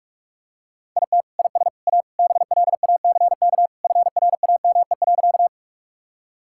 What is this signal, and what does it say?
Morse code, heard as a keyed tone, about 44 wpm, 700 Hz. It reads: IT IS A BLACK FRAME.
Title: Sentences from Top 1000 Words - Review Part 6 - 45wpm